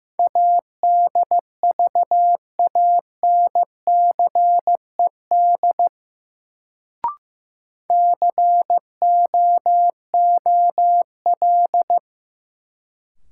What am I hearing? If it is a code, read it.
Morse code, heard as a keyed tone, about 16 wpm, 700 Hz. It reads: ADVANCED COOL
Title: Sentences from Top 1200 Words - New Word Drill - 15wpm